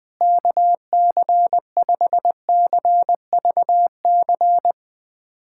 Morse code, heard as a keyed tone, about 21 wpm, 700 Hz. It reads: KC5CVC
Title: USA Callsign DE Callsign 20wpm